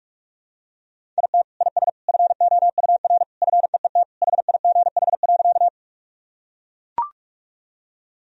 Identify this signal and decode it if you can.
Morse code, heard as a keyed tone, about 41 wpm, 700 Hz. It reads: IT IS FOUR FEET HIGH.
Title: Sentences from Top 1000 Words - Review Part 1 - 45wpm